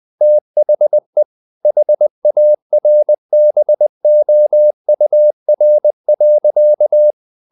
Morse code, heard as a keyed tone, about 20 wpm, 600 Hz. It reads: THE HARBOUR.
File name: the old man and the sea part 2 20wpm